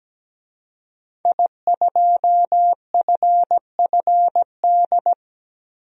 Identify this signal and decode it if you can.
Morse code, heard as a keyed tone, about 17 wpm, 700 Hz. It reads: I2FFD
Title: DXCC-100 Call Signs - 5 Characters - 17wpm